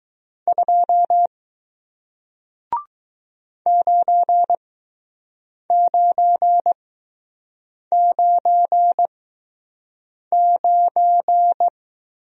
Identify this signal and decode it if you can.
Morse code, heard as a keyed tone, about 20 wpm, spaced as for 6 wpm, 700 Hz. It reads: 2 9999